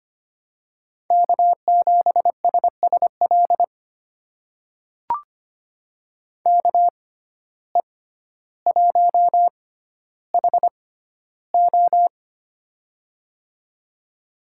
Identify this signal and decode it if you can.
Morse code, heard as a keyed tone, about 25 wpm, spaced as for 13 wpm, 700 Hz. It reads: K7SSL KE1HO
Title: USA Call Signs - 5 Characters - 6x Character Spacing - 25wpm